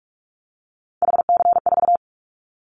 Morse code, heard as a keyed tone, about 46 wpm, 700 Hz. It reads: HC4
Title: Callsign Prefixes for Top 100 DXCC - 45wpm